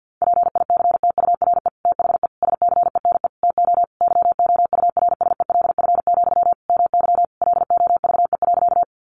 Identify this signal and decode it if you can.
Morse code, heard as a keyed tone, about 49 wpm, 700 Hz. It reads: PICTURE THE SCENE TO YOURSELF, MY LOVE.
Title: Sentences from Top 1100 Words 50wpm